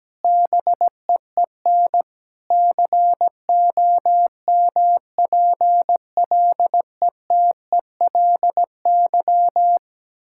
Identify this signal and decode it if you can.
Morse code, heard as a keyed tone, about 17 wpm, 700 Hz. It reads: BEEN COMPLETELY
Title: Sentences from Top 1600 Words 17wpm